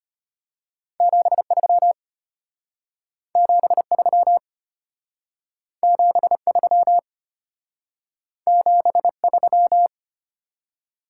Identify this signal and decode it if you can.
Morse code, heard as a keyed tone, about 33 wpm, 700 Hz. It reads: 73 73 73 73